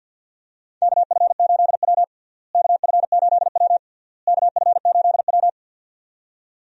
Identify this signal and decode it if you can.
Morse code, heard as a keyed tone, about 47 wpm, 700 Hz. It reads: KF8W KF8W KF8W
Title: USA Call Signs - 4 Characters - Repeated 3x - 50wpm